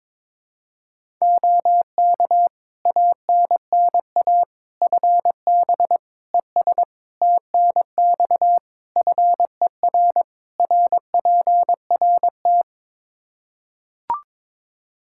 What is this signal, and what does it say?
Morse code, heard as a keyed tone, about 22 wpm, 700 Hz. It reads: OK ANNA FB ES TNX FER RPRT
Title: Practice QSO - Exchange 2 - Info - 22wpm